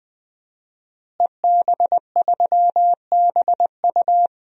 Morse code, heard as a keyed tone, about 21 wpm, 700 Hz. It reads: EB3BU
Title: DXCC-100 Call Signs - 5 Characters - Repeated 2x - 20wpm